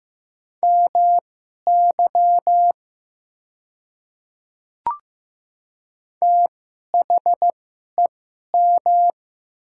Morse code, heard as a keyed tone, about 15 wpm, spaced as for 11 wpm, 690 Hz. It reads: MY THEM
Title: 2x Character Spacing - Top 100 Words - 15wpm